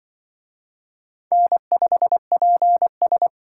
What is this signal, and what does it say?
Morse code, heard as a keyed tone, about 25 wpm, 700 Hz. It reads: N5PS
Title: Speed Racing - USA 1x2 Call Signs 20wpm